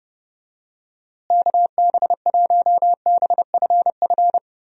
Morse code, heard as a keyed tone, about 30 wpm, 700 Hz. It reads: KB1BFF
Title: USA Call Signs - 6 Characters - Repeated 3x - 30wpm